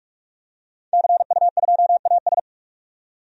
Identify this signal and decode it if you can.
Morse code, heard as a keyed tone, about 42 wpm, 700 Hz. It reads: CU2AS
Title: DXCC-100 Call Signs - 5 Characters - 45wpm